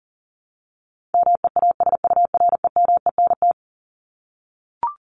Morse code, heard as a keyed tone, about 40 wpm, 700 Hz. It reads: MEASUREMENT
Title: Top 1751-2000 Words 40wpm